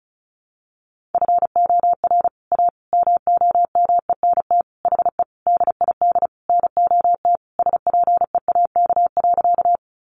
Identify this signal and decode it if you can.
Morse code, heard as a keyed tone, about 34 wpm, 700 Hz. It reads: FOR A MOMENT HE DID NOT SPEAK.
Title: Sentences from Top 1100 Words 35wpm